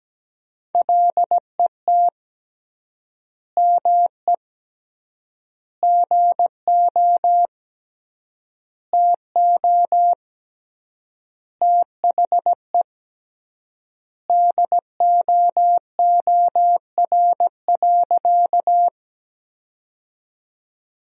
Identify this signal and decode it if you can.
Morse code, heard as a keyed tone, about 17 wpm, 700 Hz. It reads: LET ME GO TO THE DOOR.
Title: 3x Word-Spacing - Sentences from Top 300 Words - 17wpm